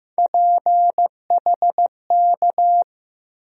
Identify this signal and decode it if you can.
Morse code, heard as a keyed tone, about 15 wpm, 700 Hz. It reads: PHK
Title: USA Call Sign Postfixes - 3 Letters 15wpm